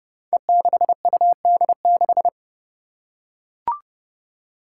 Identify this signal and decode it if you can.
Morse code, heard as a keyed tone, about 30 wpm, 700 Hz. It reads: E6UD6
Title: Five Challenging Letter-Number Combinations 30wpm